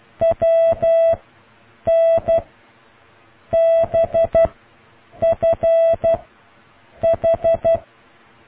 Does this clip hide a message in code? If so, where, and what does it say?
Yes: Morse code from 0.2 to 7.8 s, heard as a keyed tone, about 12 wpm, 660 Hz. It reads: W N B F H